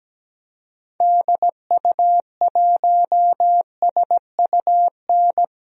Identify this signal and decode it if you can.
Morse code, heard as a keyed tone, about 17 wpm, 700 Hz. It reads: DU1SUN
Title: DXCC-100 Call Signs - 6 Characters - 17wpm